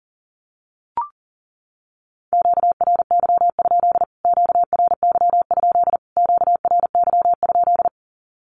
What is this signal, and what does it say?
Morse code, heard as a keyed tone, about 40 wpm, 700 Hz. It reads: QRY? QRY? QRY?
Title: Q-Codes Repeated 3x 40wpm